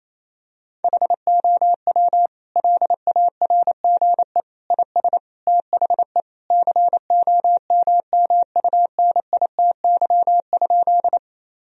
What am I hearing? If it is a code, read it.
Morse code, heard as a keyed tone, about 28 wpm, 700 Hz. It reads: HOW LARGE IS THE COMMUNITY?
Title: Sentences from Top 1000 Words - Review Part 4 - 28wpm